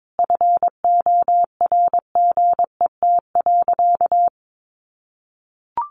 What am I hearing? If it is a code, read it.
Morse code, heard as a keyed tone, about 22 wpm, 700 Hz. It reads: FORGET.
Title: Sentences from Top 1500 Words 22wpm